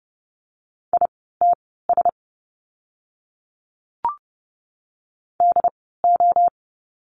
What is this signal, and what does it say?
Morse code, heard as a keyed tone, about 30 wpm, spaced as for 17 wpm, 700 Hz. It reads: ITS DO
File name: course / 30:10wpm / 108 Lesson 18f - Letter m - Words 30wpm